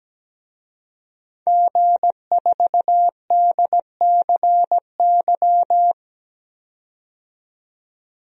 Morse code, heard as a keyed tone, about 17 wpm, 700 Hz. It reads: G4DCY
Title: DXCC-100 Call Signs - 5 Characters - 17wpm